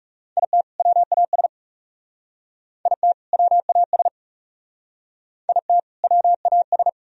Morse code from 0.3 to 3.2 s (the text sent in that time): IT WAS IT